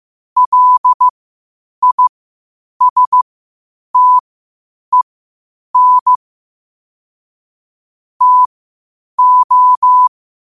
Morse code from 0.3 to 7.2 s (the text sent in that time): LISTEN